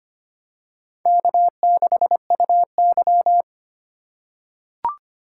Morse code, heard as a keyed tone, about 25 wpm, 700 Hz. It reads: K6UY
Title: USA Call Signs - 4 Characters - 2x Character Spacing - 25wpm